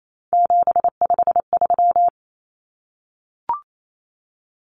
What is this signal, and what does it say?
Morse code, heard as a keyed tone, about 28 wpm, 700 Hz. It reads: 753
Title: Common Cut Numbers - 3-Digit 28wpm